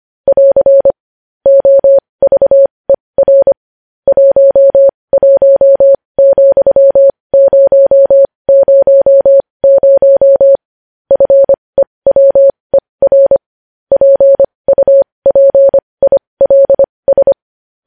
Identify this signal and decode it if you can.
Morse code, heard as a keyed tone, about 25 wpm, spaced as for 20 wpm, 550 Hz. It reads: <AR> OVER 11,000 FEWER PUPILS
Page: news-headlines-in-morse-code-at-20-wpm